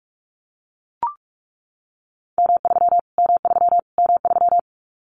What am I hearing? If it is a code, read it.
Morse code, heard as a keyed tone, about 44 wpm, 700 Hz. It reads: M3 M3 M3
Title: Callsign Prefixes for Top 25 DXCC - Repeated 3x - 45wpm